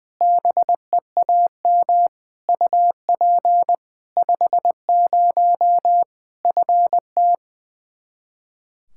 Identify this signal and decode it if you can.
Morse code, heard as a keyed tone, about 20 wpm, 700 Hz. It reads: BEAM UP 50 FT
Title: K1USN SST QSO Practice 20wpm